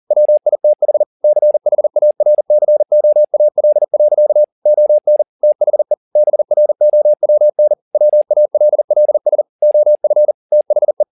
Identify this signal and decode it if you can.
Morse code, heard as a keyed tone, about 40 wpm, 600 Hz. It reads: WITH CHARCOAL. ON THE BROWN WALLS OF THE